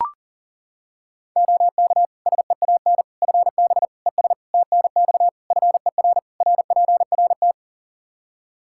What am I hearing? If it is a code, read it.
Morse code, heard as a keyed tone, about 40 wpm, 700 Hz. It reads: OK SEAN FB ES TNX FER RPRT
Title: Practice QSO - Exchange 2 - Info - 40wpm